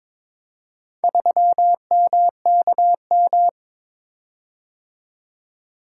Morse code, heard as a keyed tone, about 22 wpm, 700 Hz. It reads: 3MKM
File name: practice / 22wpm / Four Letter-Number Combinations - 22wpm